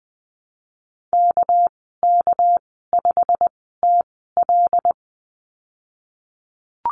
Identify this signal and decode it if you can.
Morse code, heard as a keyed tone, about 20 wpm, spaced as for 15 wpm, 700 Hz. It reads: KK5TL
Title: 200 Lesson 32g - Letter k - Call Signs 20wpm